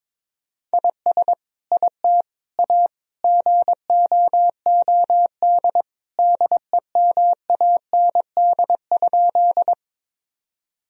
Morse code, heard as a keyed tone, about 22 wpm, 700 Hz. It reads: IS IT A GOOD DEMAND?
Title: Sentences from Top 1000 Words - Review Part 4 - 22wpm